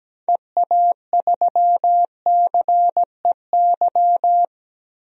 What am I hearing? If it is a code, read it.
Morse code, heard as a keyed tone, about 17 wpm, 700 Hz. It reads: EA3CEY
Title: DXCC-100 Call Signs - 6 Characters - Repeated 2x - 17wpm